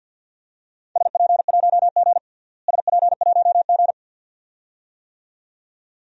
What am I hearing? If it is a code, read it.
Morse code, heard as a keyed tone, about 47 wpm, 700 Hz. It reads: SP1G SP1G
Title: DXCC-100 Call Signs - 4 Characters - Repeated 2x - 50wpm